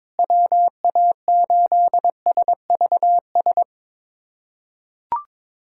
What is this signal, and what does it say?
Morse code, heard as a keyed tone, about 22 wpm, 700 Hz. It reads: WA8SVS
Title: USA Callsign DE Callsign 22wpm